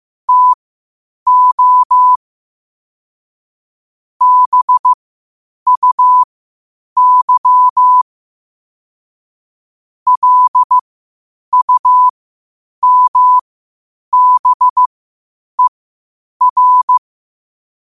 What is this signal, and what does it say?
Morse code, heard as a keyed tone, about 15 wpm, spaced as for 8 wpm, 1000 Hz. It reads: TO BUY LUMBER
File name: qotdfiles / 8wpm